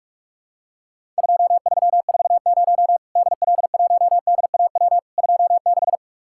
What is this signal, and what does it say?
Morse code, heard as a keyed tone, about 44 wpm, 700 Hz. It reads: 2340 DL1DAW 26
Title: Worked All Europe DX Contest - QTC Exchange - 45wpm